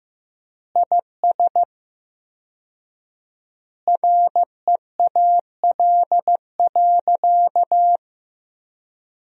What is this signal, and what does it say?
Morse code, heard as a keyed tone, about 15 wpm, 700 Hz. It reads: IS REAL.
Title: 4x Word-Spacing - Sentences from Top 300 Words - 15wpm